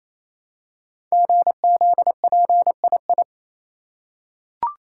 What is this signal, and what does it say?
Morse code, heard as a keyed tone, about 28 wpm, 700 Hz. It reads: GZPII
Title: Code Groups - 5 Letter:Numbers - 28wpm